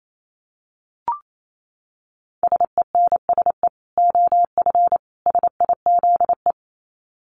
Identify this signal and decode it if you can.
Morse code, heard as a keyed tone, about 28 wpm, 700 Hz. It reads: SENSE OF SIZE
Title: Dit Challenge 28wpm